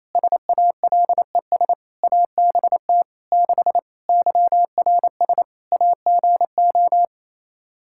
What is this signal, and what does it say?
Morse code, heard as a keyed tone, about 28 wpm, 700 Hz. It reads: SALES ABT 6 YRS AGO
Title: Practice QSO - Exchange 3 - More Info - 28wpm